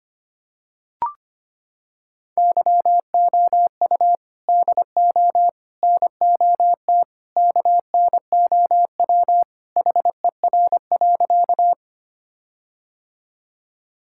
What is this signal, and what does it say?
Morse code, heard as a keyed tone, about 25 wpm, 700 Hz. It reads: YOU DO NOT KNOW HER.
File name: practice / 25wpm / Sentences with 4-Word Phrases - 25wpm